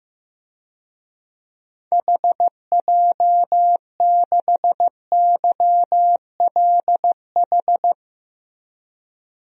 Morse code, heard as a keyed tone, about 15 wpm, 700 Hz. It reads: HJ6YLH
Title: DXCC-100 Call Signs - 6 Characters - 15wpm